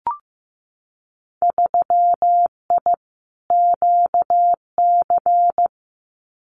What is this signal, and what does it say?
Morse code, heard as a keyed tone, about 15 wpm, 700 Hz. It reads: 3I QC
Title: Winter Field Day Exchanges Encore 15wpm